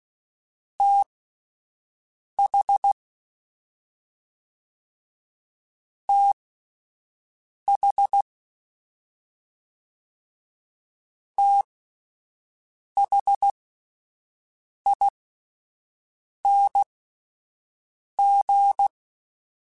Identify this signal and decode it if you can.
Morse code, heard as a keyed tone, about 16 wpm, spaced as for 5 wpm, 800 Hz. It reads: TH TH THING